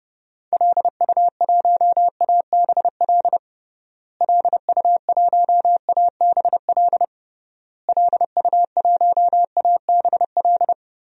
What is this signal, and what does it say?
Morse code, heard as a keyed tone, about 30 wpm, 700 Hz. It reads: LU1ABL LU1ABL LU1ABL
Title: DXCC-100 Call Signs - 6 Characters - Repeated 3x - 30wpm